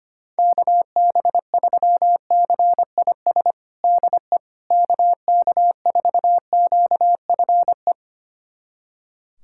Morse code, heard as a keyed tone, about 25 wpm, 700 Hz. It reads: KB3CIS DE KK4QFE
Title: USA Callsign DE Callsign - 6 Characters - 25wpm